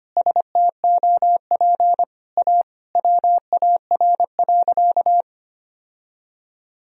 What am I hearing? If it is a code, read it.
Morse code, heard as a keyed tone, about 25 wpm, 700 Hz. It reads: STOP A WAR.
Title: Sentences from Top 1000 Words - Review Part 2 - 25wpm